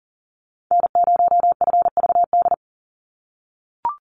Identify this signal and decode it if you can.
Morse code, heard as a keyed tone, about 40 wpm, 700 Hz. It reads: N0FVD
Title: USA Call Signs 40wpm